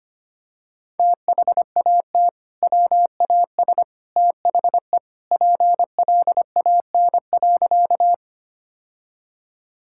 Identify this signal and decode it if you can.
Morse code, heard as a keyed tone, about 25 wpm, 700 Hz. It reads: THAT WAS THE PLAN.
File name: practice / 25wpm / Sentences with 2-Word Phrases - 25wpm